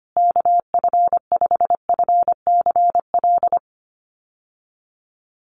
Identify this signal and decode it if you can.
Morse code, heard as a keyed tone, about 25 wpm, 700 Hz. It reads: KF5FCL
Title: USA Call Signs Repeated 2x 25wpm